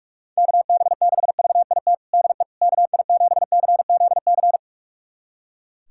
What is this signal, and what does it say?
Morse code, heard as a keyed tone, about 41 wpm, 700 Hz. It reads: KB6VIT DE KI7CZC
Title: USA Callsign DE Callsign - 6 Characters - 45wpm